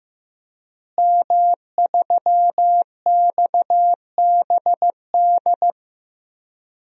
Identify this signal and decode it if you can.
Morse code, heard as a keyed tone, about 15 wpm, 700 Hz. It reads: M3XBD
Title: DXCC-100 Call Signs - 5 Characters - Repeated 3x - 15wpm